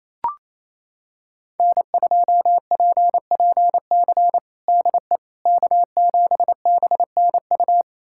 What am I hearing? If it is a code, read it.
Morse code, heard as a keyed tone, about 28 wpm, 700 Hz. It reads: N2PPC DE K7BNU
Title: USA Callsign DE Callsign - 5 Characters - 28wpm